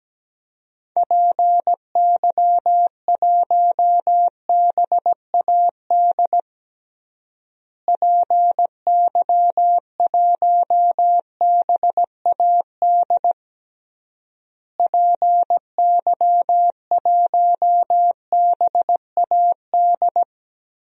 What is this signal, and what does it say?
Morse code, heard as a keyed tone, about 17 wpm, 700 Hz. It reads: PY1BAD PY1BAD PY1BAD
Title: DXCC-100 Call Signs - 6 Characters - Repeated 3x - 17wpm